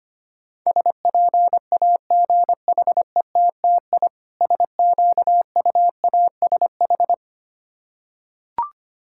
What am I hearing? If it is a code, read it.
Morse code, heard as a keyed tone, about 25 wpm, 700 Hz. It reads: SPAGHETTI SQUASH
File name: practice / 25wpm / Difficult Words with Hints - 25wpm